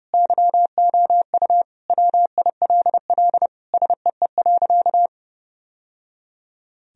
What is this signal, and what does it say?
Morse code, heard as a keyed tone, about 30 wpm, 700 Hz. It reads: YOU WILL SEE.
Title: Sentences from Top 1000 Words - Review Part 1 - 30wpm